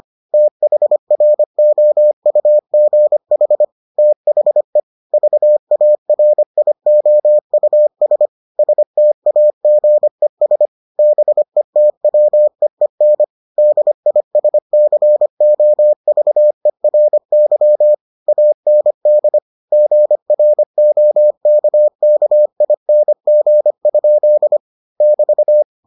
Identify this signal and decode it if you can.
Morse code, heard as a keyed tone, about 25 wpm, 600 Hz. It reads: THROUGH THE VARIOUS STAGES BETWEEN DISCOVERY AND GROKKING? <BT>